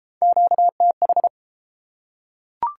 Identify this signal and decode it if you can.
Morse code, heard as a keyed tone, about 28 wpm, 700 Hz. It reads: QTH